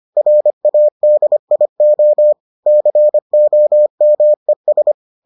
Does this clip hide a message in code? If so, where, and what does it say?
Yes: Morse code from 0.2 to 4.9 s, heard as a keyed tone, about 25 wpm, 600 Hz. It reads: RADIO COMES